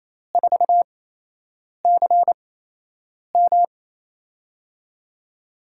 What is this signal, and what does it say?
Morse code, heard as a keyed tone, about 28 wpm, spaced as for 8 wpm, 700 Hz. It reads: 4CM
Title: USA Call Signs - 8x Character Spacing - 28wpm